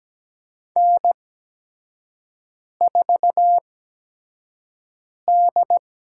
Morse code, heard as a keyed tone, about 17 wpm, spaced as for 5 wpm, 700 Hz. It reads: N4D
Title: USA Call Signs - 5 Characters - 8x Character Spacing - 17wpm